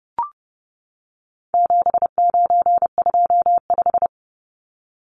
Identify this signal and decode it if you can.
Morse code, heard as a keyed tone, about 30 wpm, 700 Hz. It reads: 7925
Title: Numbers - 4-Digit 30wpm